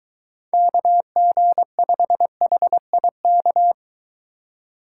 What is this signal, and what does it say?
Morse code, heard as a keyed tone, about 22 wpm, 700 Hz. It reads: KG5HIK